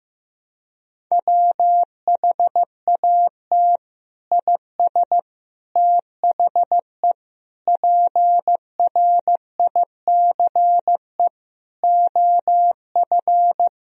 Morse code, heard as a keyed tone, about 15 wpm, 700 Hz. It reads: WHAT IS THE PRICE OF
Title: Sentences from Top 1000 Words - Review Part 4 - 15wpm